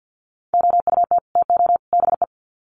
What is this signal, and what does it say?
Morse code, heard as a keyed tone, about 50 wpm, 700 Hz. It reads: OUT TO BE
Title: Sets of 3 Words 50wpm